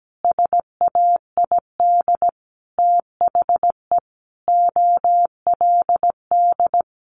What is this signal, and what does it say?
Morse code, heard as a keyed tone, about 17 wpm, 700 Hz. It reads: SAID THE OLD
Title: Sentences from Top 1100 Words 17wpm